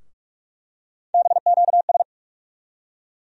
Morse code, heard as a keyed tone, about 47 wpm, 700 Hz. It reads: BQS
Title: USA Call Sign Postfixes - 3 Letters 45wpm